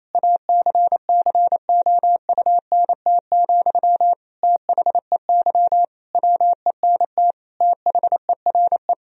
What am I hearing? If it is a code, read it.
Morse code, heard as a keyed tone, about 28 wpm, 700 Hz. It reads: ACCOUNT, THEY WENT THERE
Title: Sentences from Top 1000 Words - Review Part 4 - 28wpm